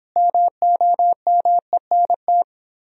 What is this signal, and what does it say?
Morse code, heard as a keyed tone, about 29 wpm, 700 Hz. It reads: MOMENT